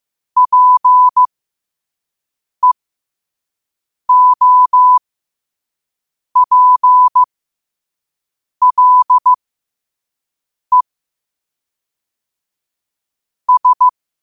Morse code, heard as a keyed tone, about 15 wpm, spaced as for 5 wpm, 1000 Hz. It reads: PEOPLE S
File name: qotdfiles / 5wpm